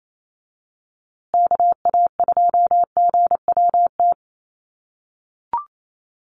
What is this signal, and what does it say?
Morse code, heard as a keyed tone, about 28 wpm, 700 Hz. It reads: KA2GWT
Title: USA Call Signs Repeated 2x 28wpm